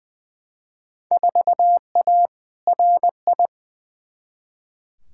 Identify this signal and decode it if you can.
Morse code, heard as a keyed tone, about 21 wpm, 700 Hz. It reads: 4A RI
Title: Field Day Exchanges Encore - 20wpm